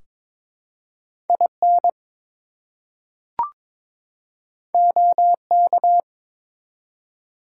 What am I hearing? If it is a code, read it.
Morse code, heard as a keyed tone, about 22 wpm, 700 Hz. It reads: IN OK
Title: Two Letter Words - 22wpm